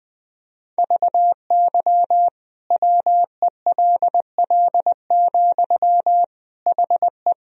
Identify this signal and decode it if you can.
Morse code, heard as a keyed tone, about 20 wpm, 700 Hz. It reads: VY WELL, HE
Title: Sentences from Top 200 Words and Abbreviations 20wpm